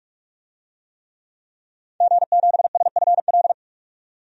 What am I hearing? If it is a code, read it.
Morse code, heard as a keyed tone, about 46 wpm, 700 Hz. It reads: G7SFL
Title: DXCC-100 Call Signs - 5 Characters - 45wpm